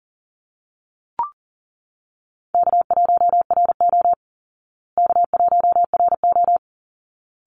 Morse code, heard as a keyed tone, about 40 wpm, 700 Hz. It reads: K1RO K1RO
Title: Call Signs Repeated 3x - 40wpm